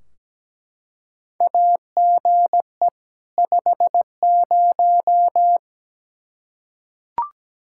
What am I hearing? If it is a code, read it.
Morse code, heard as a keyed tone, about 17 wpm, 700 Hz. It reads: AGE 50
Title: Practice QSO - Exchange 3 - More Info - 17wpm